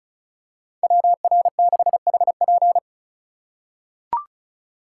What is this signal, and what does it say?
Morse code, heard as a keyed tone, about 36 wpm, 700 Hz. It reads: WR6HP
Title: USA Call Signs - 2x Character Spacing - 35wpm